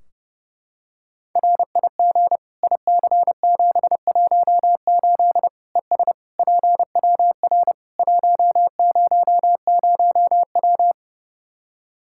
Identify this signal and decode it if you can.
Morse code, heard as a keyed tone, about 29 wpm, 700 Hz. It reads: RIG IC718 ES PWR 100W
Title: Practice QSO - Exchange 2 - Info - 30wpm